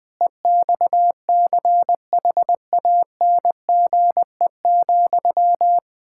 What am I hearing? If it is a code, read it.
Morse code, heard as a keyed tone, about 20 wpm, 700 Hz. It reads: EXCHANGE,
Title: Sentences from Top 1000 Words - Review Part 6 - 20wpm